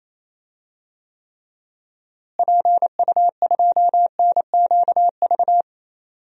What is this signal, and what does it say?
Morse code, heard as a keyed tone, about 28 wpm, 700 Hz. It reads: PU2NQV
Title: DXCC-100 Call Signs - 6 Characters - Repeated 3x - 28wpm